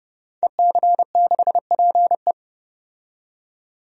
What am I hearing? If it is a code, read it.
Morse code, heard as a keyed tone, about 32 wpm, 700 Hz. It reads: EC6PE
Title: DXCC-100 Call Signs - 5 Characters - 30wpm